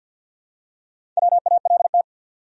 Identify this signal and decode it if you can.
Morse code, heard as a keyed tone, about 51 wpm, 700 Hz. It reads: WALT
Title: USA Common Names - Max 5 Letters - 50wpm